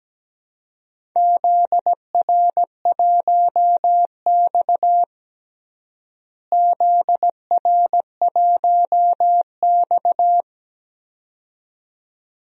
Morse code, heard as a keyed tone, about 17 wpm, 700 Hz. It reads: ZR1X ZR1X
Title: DXCC-100 Call Signs - 4 Characters - Repeated 2x - 17wpm